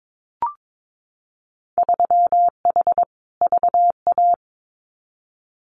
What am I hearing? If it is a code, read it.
Morse code, heard as a keyed tone, about 22 wpm, 700 Hz. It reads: 3H VA
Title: Winter Field Day Exchanges Encore 22wpm